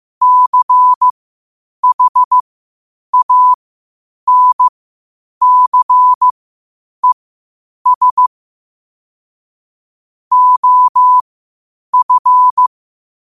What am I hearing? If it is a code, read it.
Morse code, heard as a keyed tone, about 15 wpm, spaced as for 8 wpm, 1000 Hz. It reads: CHANCES OF
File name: qotdfiles / 8wpm